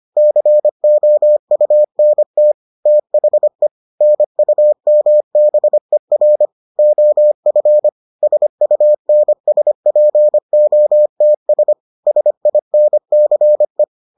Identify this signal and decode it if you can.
Morse code, heard as a keyed tone, about 25 wpm, 600 Hz. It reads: COUNT THE NUMBER OF SUNSPOTS SINCE